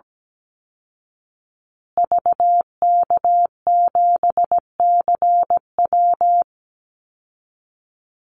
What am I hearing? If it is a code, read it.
Morse code, heard as a keyed tone, about 17 wpm, 700 Hz. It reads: VK7CW
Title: Call Signs Repeated 2x - 17wpm